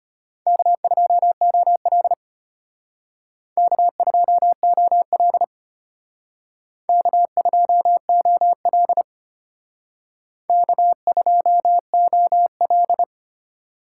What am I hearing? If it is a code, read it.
Morse code, heard as a keyed tone, about 33 wpm, 700 Hz. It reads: K2OL K2OL K2OL K2OL